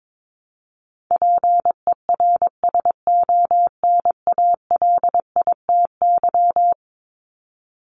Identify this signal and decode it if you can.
Morse code, heard as a keyed tone, about 22 wpm, 700 Hz. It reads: PERSONALITY
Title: Top 1251-1500 Words 22wpm